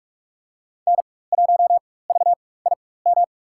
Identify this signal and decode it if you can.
Morse code, heard as a keyed tone, about 47 wpm, spaced as for 21 wpm, 700 Hz. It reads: N1VIM
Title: USA Call Signs - 5 Characters - 4x Character Spacing - 45wpm